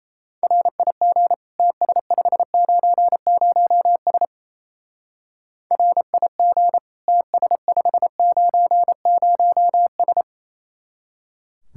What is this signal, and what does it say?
Morse code, heard as a keyed tone, about 34 wpm, 700 Hz. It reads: RIG TS590S RIG TS590S
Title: Speed-Racing Common QSO Phrases - 28wpm